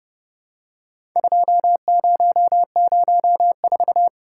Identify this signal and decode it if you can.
Morse code, heard as a keyed tone, about 30 wpm, 700 Hz. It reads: 2004
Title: Worked All Europe DX Contest - QTC Exchange - 30wpm